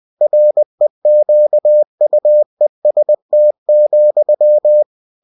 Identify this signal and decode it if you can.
Morse code, heard as a keyed tone, about 20 wpm, 600 Hz. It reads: REQUEST,
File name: cw-books / winnie-the-pooh / winnie-20wpm-page-0116